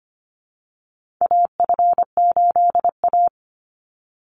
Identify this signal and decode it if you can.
Morse code, heard as a keyed tone, about 25 wpm, 700 Hz. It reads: AF8A
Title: Call Signs - 25wpm